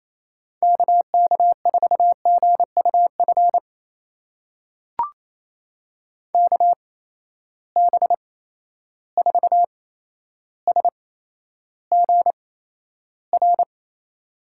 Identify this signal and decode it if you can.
Morse code, heard as a keyed tone, about 28 wpm, spaced as for 12 wpm, 700 Hz. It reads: KK4GUF KB4SGR
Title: USA Call Signs - 6 Characters - 8x Character Spacing - 28wpm